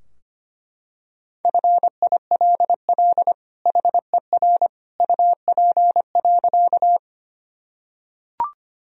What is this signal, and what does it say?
Morse code, heard as a keyed tone, about 25 wpm, 700 Hz. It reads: FILL HER UP.
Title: Sentences from Top 1000 Words - Review Part 3 - 25wpm